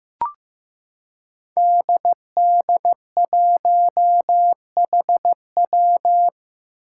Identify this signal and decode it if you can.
Morse code, heard as a keyed tone, about 15 wpm, 700 Hz. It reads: DD1HW
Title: Five Challenging Letter-Number Combinations 15wpm